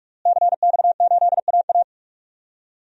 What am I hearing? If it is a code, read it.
Morse code, heard as a keyed tone, about 42 wpm, 700 Hz. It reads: CX8AA
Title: DXCC-100 Call Signs - 5 Characters - Repeated 3x - 45wpm